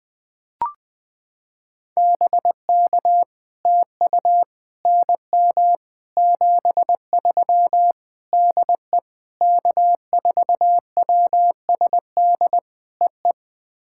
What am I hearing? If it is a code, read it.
Morse code, heard as a keyed tone, about 20 wpm, 700 Hz. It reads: BK TU NM 73 DE K4WSD EE
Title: POTA QSO Practice - 20wpm